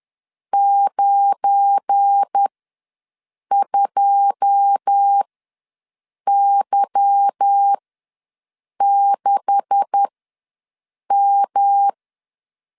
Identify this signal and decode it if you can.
Morse code, heard as a keyed tone, about 10 wpm, spaced as for 6 wpm, 800 Hz. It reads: 92Y6M
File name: LEZIONI / 9 286 ws 10wpm 4032